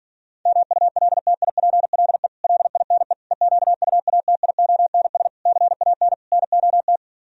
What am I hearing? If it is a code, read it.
Morse code, heard as a keyed tone, about 46 wpm, 700 Hz. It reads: MULTIPLE LINE EQUATIONS CAN NOT